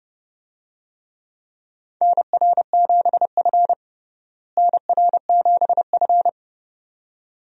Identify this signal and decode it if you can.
Morse code, heard as a keyed tone, about 29 wpm, 700 Hz. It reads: NR7F NR7F
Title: USA Call Signs - 4 Characters - Repeated 2x - 30wpm